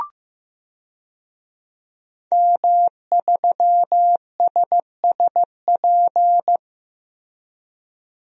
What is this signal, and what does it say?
Morse code, heard as a keyed tone, about 15 wpm, 700 Hz. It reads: M3SSP
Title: DXCC-100 Call Signs - 5 Characters - Repeated 3x - 15wpm